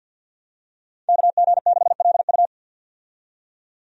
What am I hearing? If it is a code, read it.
Morse code, heard as a keyed tone, about 52 wpm, 700 Hz. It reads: KG6LU
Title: USA Call Signs - 5 Characters - 50wpm